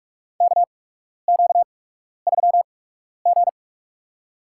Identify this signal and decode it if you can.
Morse code, heard as a keyed tone, about 47 wpm, spaced as for 12 wpm, 700 Hz. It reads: KQ3G
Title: USA Call Signs - 4 Characters - 8x Character Spacing - 45wpm